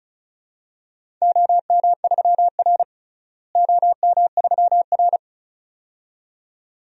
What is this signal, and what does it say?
Morse code, heard as a keyed tone, about 36 wpm, 700 Hz. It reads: OM3R OM3R
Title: DXCC-100 Call Signs - 4 Characters - Repeated 2x - 35wpm